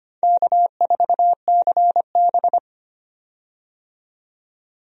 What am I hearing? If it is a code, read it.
Morse code, heard as a keyed tone, about 25 wpm, 700 Hz. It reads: K4CB
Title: USA Call Signs - 4 Characters - 25wpm